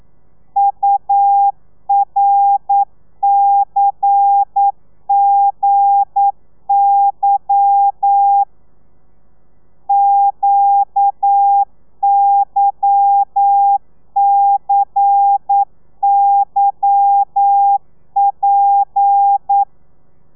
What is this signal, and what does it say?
Morse code, heard as a keyed tone, about 9 wpm, 800 Hz. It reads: URCGY QYCYP